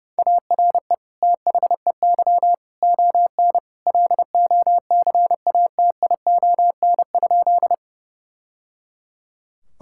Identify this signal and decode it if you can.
Morse code, heard as a keyed tone, about 30 wpm, 700 Hz. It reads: ARE THEY ON LOCATION?
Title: Sentences from Top 1000 Words - Review Part 6 - 30wpm